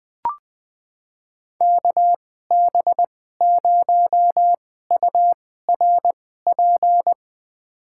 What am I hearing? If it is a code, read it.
Morse code, heard as a keyed tone, about 21 wpm, spaced as for 15 wpm, 700 Hz. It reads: KB0URP
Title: USA Call Signs - 2x Character Spacing - 20wpm